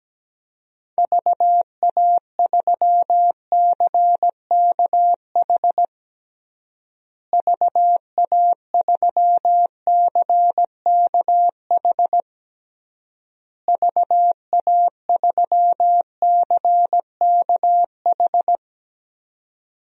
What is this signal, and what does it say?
Morse code, heard as a keyed tone, about 17 wpm, 700 Hz. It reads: VA3CKH VA3CKH VA3CKH
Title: DXCC-100 Call Signs - 6 Characters - Repeated 3x - 17wpm